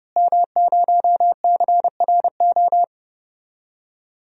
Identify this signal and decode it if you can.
Morse code, heard as a keyed tone, about 31 wpm, 700 Hz. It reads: M0CRO
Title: DXCC-100 Call Signs - 5 Characters - Repeated 2x - 30wpm